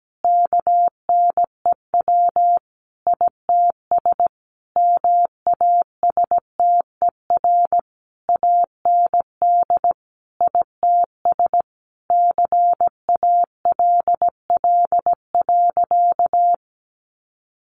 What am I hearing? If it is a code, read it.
Morse code, heard as a keyed tone, about 17 wpm, 700 Hz. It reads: KNEW ITS MASTER AND ITS CALL.
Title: Sentences from Top 1300 Words 17wpm